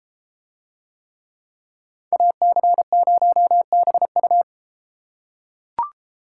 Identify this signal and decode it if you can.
Morse code, heard as a keyed tone, about 30 wpm, 700 Hz. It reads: AC0BU